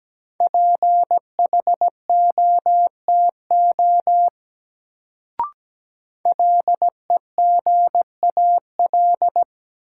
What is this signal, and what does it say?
Morse code, heard as a keyed tone, about 17 wpm, 700 Hz. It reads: PHOTO LEGAL
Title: Sentences from Top 1200 Words - New Word Drill - 17wpm